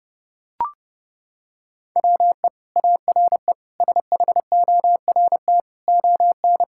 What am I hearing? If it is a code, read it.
Morse code, heard as a keyed tone, about 29 wpm, 700 Hz. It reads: WE ARE SHORT ON
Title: Sentences from Top 1000 Words - Review Part 5 - 30wpm